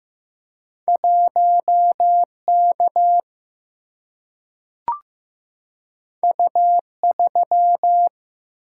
Morse code, heard as a keyed tone, about 15 wpm, 700 Hz. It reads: 1K U3
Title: Single-Letter Single-Number 15wpm